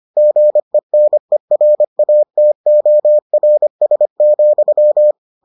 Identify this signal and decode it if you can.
Morse code, heard as a keyed tone, about 25 wpm, 600 Hz. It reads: GENERATORS,